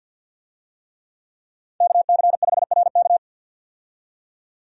Morse code, heard as a keyed tone, about 51 wpm, 700 Hz. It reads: KC5RK
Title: USA Call Signs - 5 Characters - 50wpm